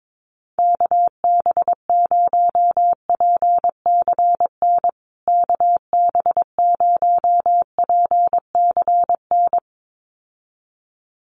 Morse code, heard as a keyed tone, about 22 wpm, 700 Hz. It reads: KB0PCN KB0PCN
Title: USA Call Signs Repeated 2x 22wpm